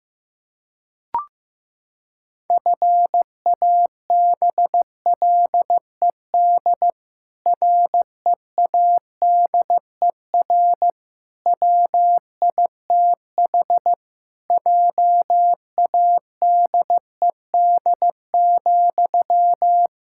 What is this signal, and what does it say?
Morse code, heard as a keyed tone, about 15 wpm, 700 Hz. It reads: FABLED READER WITH JADED,
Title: Pangram Challenge v2 - 15wpm